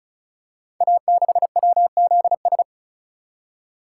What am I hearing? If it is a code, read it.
Morse code, heard as a keyed tone, about 36 wpm, 690 Hz. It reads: A6WZS
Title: Code Groups - 5 Letter:Numbers - 35wpm